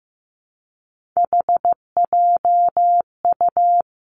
Morse code, heard as a keyed tone, about 15 wpm, 700 Hz. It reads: HJU
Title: Three Challenging Letter-Number Combinations 15wpm